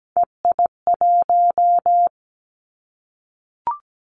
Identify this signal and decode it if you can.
Morse code, heard as a keyed tone, about 17 wpm, 700 Hz. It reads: EI1
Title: Callsign Prefixes for Top 50 DXCC - 17wpm